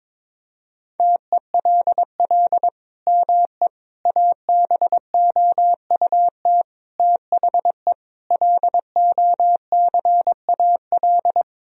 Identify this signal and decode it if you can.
Morse code, heard as a keyed tone, about 22 wpm, 700 Hz. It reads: TELL ME ABOUT THE LOCAL
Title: Sentences from Top 1000 Words - Review Part 3 - 22wpm